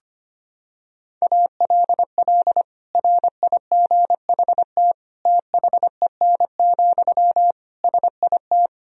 Morse code, heard as a keyed tone, about 25 wpm, 700 Hz. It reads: ALL RIGHT THEN, SIT
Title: Sentences from Top 500 Words and Abbreviations 25wpm